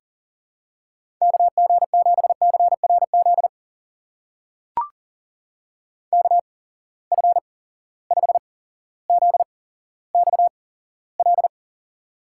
Morse code, heard as a keyed tone, about 39 wpm, spaced as for 17 wpm, 700 Hz. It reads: KG7CRZ KF5ZXL
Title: USA Call Signs - 6 Characters - 8x Character Spacing - 40wpm